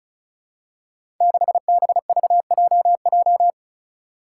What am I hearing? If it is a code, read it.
Morse code, heard as a keyed tone, about 36 wpm, 700 Hz. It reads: 6BVJJ
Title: Five Challenging Letter-Number Combinations 35wpm